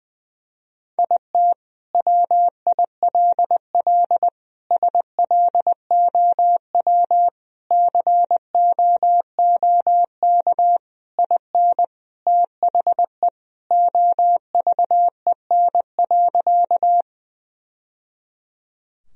Morse code, heard as a keyed tone, about 20 wpm, 700 Hz. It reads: IT WILL SLOW COOK IN THE OVEN.
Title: Sentences from Top 1000 Words - Review Part 4 - 20wpm